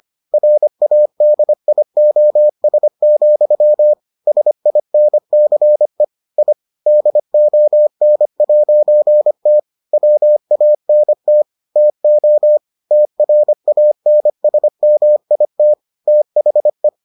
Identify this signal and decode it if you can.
Morse code, heard as a keyed tone, about 25 wpm, 600 Hz. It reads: RADIOS, SINCE I DON'T WANT TO TRANSMIT THE